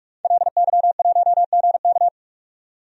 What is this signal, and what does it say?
Morse code, heard as a keyed tone, about 44 wpm, 690 Hz. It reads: LY1GK
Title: DXCC-100 Call Signs - 5 Characters - 45wpm